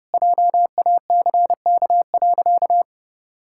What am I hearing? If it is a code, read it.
Morse code, heard as a keyed tone, about 32 wpm, 700 Hz. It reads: JACK.
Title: Pangram Challenge v2 - 30wpm